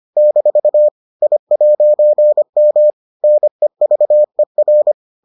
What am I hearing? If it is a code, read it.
Morse code, heard as a keyed tone, about 25 wpm, 600 Hz. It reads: - I'M NEVER